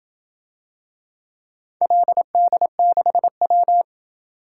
Morse code, heard as a keyed tone, about 27 wpm, 700 Hz. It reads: LD6W